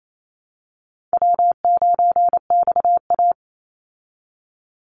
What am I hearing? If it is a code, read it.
Morse code, heard as a keyed tone, about 28 wpm, 700 Hz. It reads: W9XA
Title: Easy Call Signs Repeated 3x - 28wpm